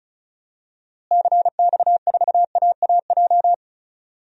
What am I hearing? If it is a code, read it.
Morse code, heard as a keyed tone, about 36 wpm, 700 Hz. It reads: CX4AAJ
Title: DXCC-100 Call Signs - 6 Characters - Repeated 3x - 35wpm